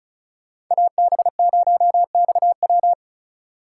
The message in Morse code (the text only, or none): AB0XW